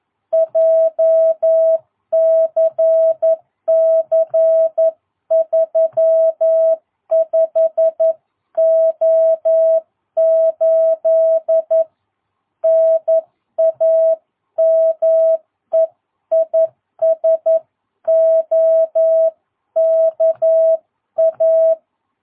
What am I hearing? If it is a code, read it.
Morse code, heard as a keyed tone, about 11 wpm, 660 Hz. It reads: JCC 35O8 NAME IS OKA
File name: jh8cbh / 2022cw